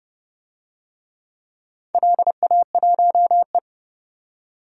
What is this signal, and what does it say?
Morse code, heard as a keyed tone, about 30 wpm, 700 Hz. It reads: LA1E
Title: DXCC-100 Call Signs - 4 Characters - 30wpm